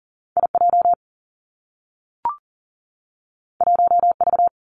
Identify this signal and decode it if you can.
Morse code, heard as a keyed tone, about 42 wpm, 700 Hz. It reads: IJ 1V
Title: Two Challenging Letter-Number Combinations 40wpm